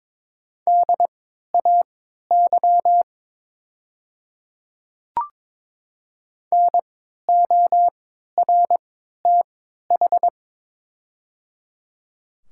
Morse code, heard as a keyed tone, about 23 wpm, spaced as for 13 wpm, 700 Hz. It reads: DAY NORTH
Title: 3x Character Spacing - Top 300 Words - 22wpm